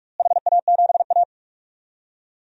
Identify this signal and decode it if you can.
Morse code, heard as a keyed tone, about 42 wpm, 700 Hz. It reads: HA7A
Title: DXCC-100 Call Signs - 4 Characters - Repeated 3x - 45wpm